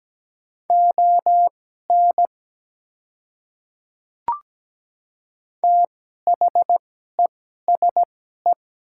Morse code, heard as a keyed tone, about 17 wpm, spaced as for 12 wpm, 700 Hz. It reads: ON THESE
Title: 2x Character Spacing - Top 100 Words - 17wpm